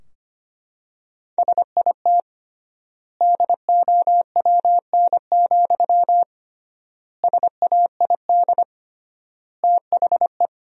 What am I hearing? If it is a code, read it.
Morse code, heard as a keyed tone, about 25 wpm, 700 Hz. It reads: SIT DOWN, SAID THE
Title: 3x Word-Spacing - Sentences from Top 500 Words - 25wpm